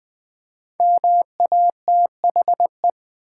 Morse code, heard as a keyed tone, about 20 wpm, 700 Hz. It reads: MATHE
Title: Build A Word - 5 Letters At A Time - 12-Letter Words - 20wpm